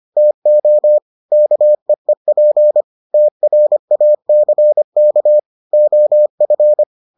Morse code, heard as a keyed tone, about 25 wpm, 600 Hz. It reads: TO KEEP TRACK OF